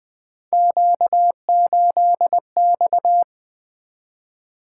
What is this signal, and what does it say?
Morse code, heard as a keyed tone, about 20 wpm, 700 Hz. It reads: Q8X
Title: Three Letter-Number Combinations - 20wpm